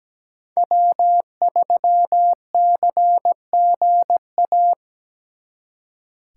Morse code, heard as a keyed tone, about 20 wpm, 700 Hz. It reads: W3CGA
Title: Speed Racing - USA 1x3 Call Signs 17wpm